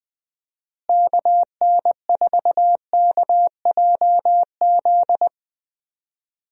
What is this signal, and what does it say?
Morse code, heard as a keyed tone, about 20 wpm, 700 Hz. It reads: KN4KJZ
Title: USA Call Signs - 6 Characters - Repeated 2x - 20wpm